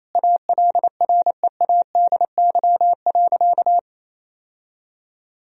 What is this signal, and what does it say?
Morse code, heard as a keyed tone, about 28 wpm, 700 Hz. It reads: ALREADY.
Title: Sentences from Top 1000 Words - Review Part 3 - 28wpm